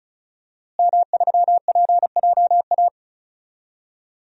Morse code, heard as a keyed tone, about 36 wpm, 700 Hz. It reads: M3PJA
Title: DXCC-100 Call Signs - 5 Characters - Repeated 2x - 35wpm